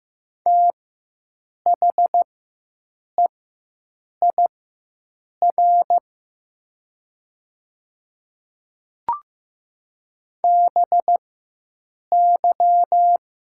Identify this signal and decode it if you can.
Morse code, heard as a keyed tone, about 15 wpm, spaced as for 7 wpm, 700 Hz. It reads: THEIR BY
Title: 4x Character Spacing - Top 200 Words - 15wpm